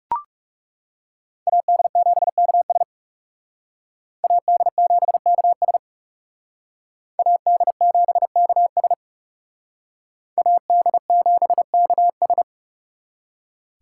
Morse code, heard as a keyed tone, about 39 wpm, 700 Hz. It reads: AD7KS AD7KS AD7KS AD7KS